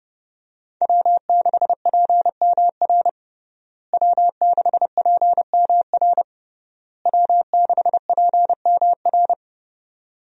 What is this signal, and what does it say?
Morse code, heard as a keyed tone, about 32 wpm, 700 Hz. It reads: W6PMR W6PMR W6PMR
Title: USA Call Signs - 5 Characters - Repeated 3x - 30wpm